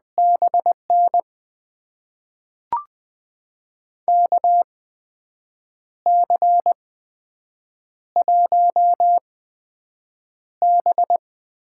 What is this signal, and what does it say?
Morse code, heard as a keyed tone, about 21 wpm, spaced as for 7 wpm, 700 Hz. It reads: BN KC1B
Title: USA Call Signs - 6 Characters - 8x Character Spacing - 20wpm